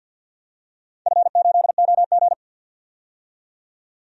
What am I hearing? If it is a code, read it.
Morse code, heard as a keyed tone, about 48 wpm, 700 Hz. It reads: F8OG
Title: DXCC-100 Call Signs - 4 Characters - 50wpm